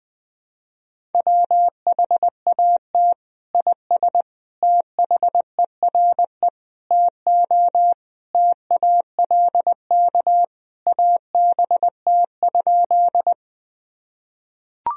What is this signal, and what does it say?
Morse code, heard as a keyed tone, about 20 wpm, 700 Hz. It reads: WHAT IS THERE TO TALK ABT?
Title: Sentences from Top 400 Words and Abbreviations 20wpm